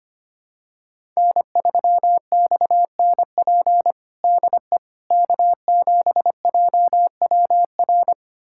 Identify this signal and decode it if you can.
Morse code, heard as a keyed tone, about 25 wpm, 700 Hz. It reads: N3XNP DE K7JWR
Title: USA Callsign DE Callsign - 5 Characters - 25wpm